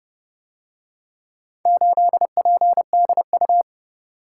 Morse code, heard as a keyed tone, about 27 wpm, 700 Hz. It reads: 8PDU